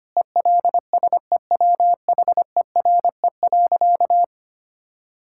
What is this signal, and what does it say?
Morse code, heard as a keyed tone, about 25 wpm, 700 Hz. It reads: ELSEWHERE.
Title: Sentences from Top 1600 Words 25wpm